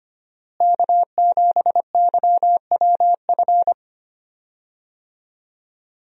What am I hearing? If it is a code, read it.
Morse code, heard as a keyed tone, about 25 wpm, 700 Hz. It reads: K7YWF
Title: USA Call Signs - 5 Characters - Repeated 2x - 25wpm